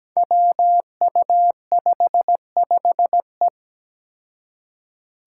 Morse code, heard as a keyed tone, about 17 wpm, 700 Hz. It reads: WU55E
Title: Five Challenging Letter-Number Combinations 17wpm